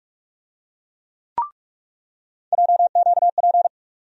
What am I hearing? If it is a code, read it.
Morse code, heard as a keyed tone, about 42 wpm, 700 Hz. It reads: JQP
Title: USA Call Sign Postfixes - 3 Letters 45wpm